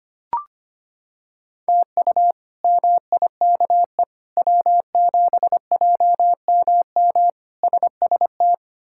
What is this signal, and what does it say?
Morse code, heard as a keyed tone, about 25 wpm, 700 Hz. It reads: TU MIKE W7JMM SST
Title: K1USN SST QSO Practice - 25wpm